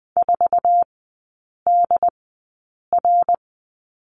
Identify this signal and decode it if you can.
Morse code, heard as a keyed tone, about 21 wpm, spaced as for 8 wpm, 690 Hz. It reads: 4DR
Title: 145 Lesson 24e - Letter y - Three Characters 20wpm